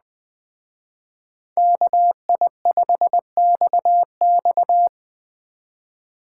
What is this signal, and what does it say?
Morse code, heard as a keyed tone, about 20 wpm, 700 Hz. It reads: KI5XX